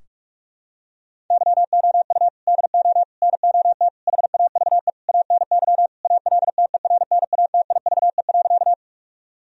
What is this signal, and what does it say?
Morse code, heard as a keyed tone, about 41 wpm, 700 Hz. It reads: YOU DO NOT HAVE ANY ALTERNATIVE.
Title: Sentences from Top 1000 Words - Review Part 4 - 45wpm